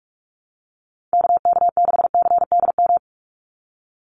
Morse code, heard as a keyed tone, about 44 wpm, 700 Hz. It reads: KK6CDM
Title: USA Call Signs 45wpm